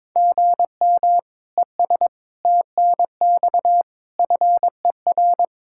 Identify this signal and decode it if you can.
Morse code, heard as a keyed tone, about 22 wpm, 700 Hz. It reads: GM ES TNX FER
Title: Practice QSO - Exchange 1 - Beginning - 22wpm